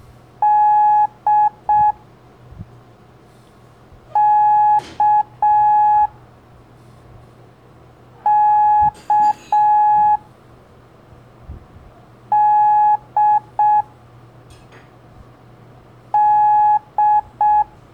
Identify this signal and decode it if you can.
Morse code, heard as a keyed tone, about 6 wpm, 840 Hz. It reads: D K K D D
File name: blog / LETRAS COM SOM SEMELHANTE MORSE